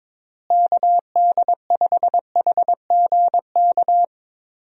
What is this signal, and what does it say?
Morse code, heard as a keyed tone, about 23 wpm, 700 Hz. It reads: KD5HGK
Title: USA Call Signs - 6 Characters - Repeated 3x - 22wpm